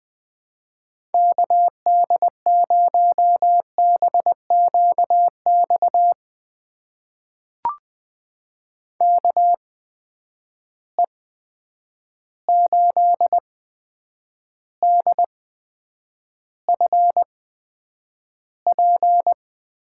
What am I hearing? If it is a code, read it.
Morse code, heard as a keyed tone, about 20 wpm, spaced as for 9 wpm, 700 Hz. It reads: KD0BQX KE8DFP
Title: USA Call Signs - 6 Characters - 8x Character Spacing - 20wpm